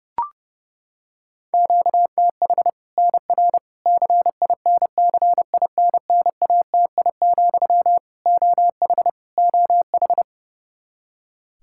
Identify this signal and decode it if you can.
Morse code, heard as a keyed tone, about 30 wpm, 700 Hz. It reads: QTH NR CINCINNATI, OH OH
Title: Practice QSO - Exchange 1 - Beginning - 30wpm